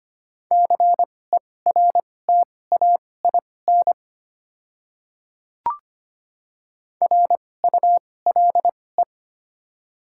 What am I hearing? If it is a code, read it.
Morse code, heard as a keyed tone, about 25 wpm, spaced as for 18 wpm, 700 Hz. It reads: CERTAIN RULE
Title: 2x Character Spacing - Top 500 Words - 25wpm